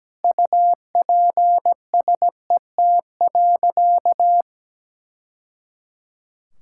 Morse code, heard as a keyed tone, about 17 wpm, 700 Hz. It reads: UPSET.
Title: Sentences from Top 1000 Words - Review Part 5 - 17wpm